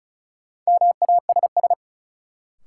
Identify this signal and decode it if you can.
Morse code, heard as a keyed tone, about 36 wpm, 700 Hz. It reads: MASS
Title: Sets of 5 Words Encore - New Word Drill - 35wpm